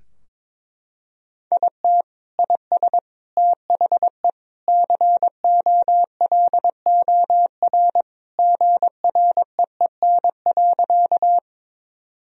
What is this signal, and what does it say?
Morse code, heard as a keyed tone, about 22 wpm, 700 Hz. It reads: IT IS THE COLOR GREEN.
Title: Sentences from Top 1000 Words - Review Part 2 - 22wpm